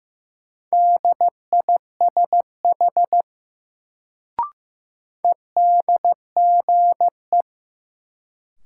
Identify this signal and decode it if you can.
Morse code, heard as a keyed tone, about 15 wpm, 700 Hz. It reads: DISH EDGE
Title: Sentences from Top 1300 Words - New Word Drill - 15wpm